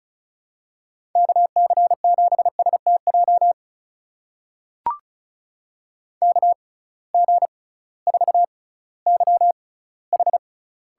Morse code, heard as a keyed tone, about 36 wpm, spaced as for 19 wpm, 700 Hz. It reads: KC7STJ KG4YH
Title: USA Call Signs - 6 Characters - 6x Character Spacing - 35wpm